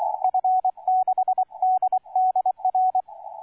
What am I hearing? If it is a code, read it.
Morse code, heard as a keyed tone, about 23 wpm, 750 Hz, with radio noise. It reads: F6DDR